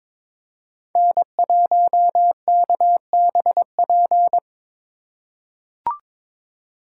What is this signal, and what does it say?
Morse code, heard as a keyed tone, about 21 wpm, 700 Hz. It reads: N1KBP
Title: USA Call Signs - 2x Character Spacing - 22wpm